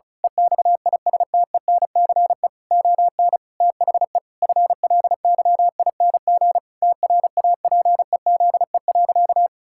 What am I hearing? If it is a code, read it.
Morse code, heard as a keyed tone, about 36 wpm, 700 Hz. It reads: EXISTENCE ON THE FLYING TRAPEZE.
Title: Pangram Challenge v2 - 35wpm